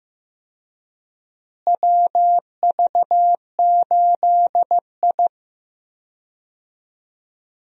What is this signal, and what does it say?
Morse code, heard as a keyed tone, about 15 wpm, 700 Hz. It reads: WV8I
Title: USA Call Signs - 4 Characters - Repeated 3x - 15wpm